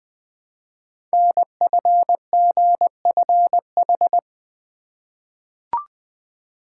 Morse code, heard as a keyed tone, about 20 wpm, 700 Hz. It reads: NFGFH
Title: Code Groups - 5 Letters - 20wpm